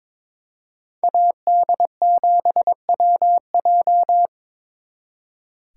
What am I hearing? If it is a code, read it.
Morse code, heard as a keyed tone, about 26 wpm, 700 Hz. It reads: AD7WJ
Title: Speed Racing - USA 2x2 Call Signs 22wpm